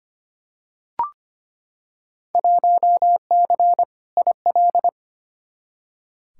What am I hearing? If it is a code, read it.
Morse code, heard as a keyed tone, about 25 wpm, 700 Hz. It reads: 1C IL
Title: Field Day Exchanges - 25wpm